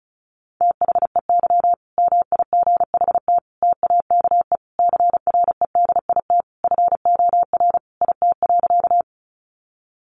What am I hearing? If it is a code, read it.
Morse code, heard as a keyed tone, about 34 wpm, 700 Hz. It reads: THEY MIGHT TAKE CREDIT FOR IT.
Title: Sentences from Top 1000 Words 35wpm